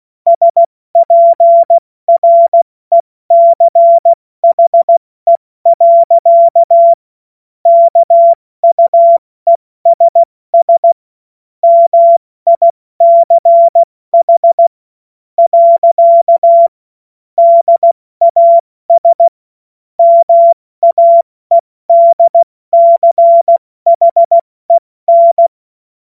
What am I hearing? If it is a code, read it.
Morse code, heard as a keyed tone, about 16 wpm, spaced as for 14 wpm, 680 Hz. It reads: SPRECHE. KUESS MICH. DAS MAEDCHEN
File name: Morsekurs / Witze / Witz 08. November 2022 70 F80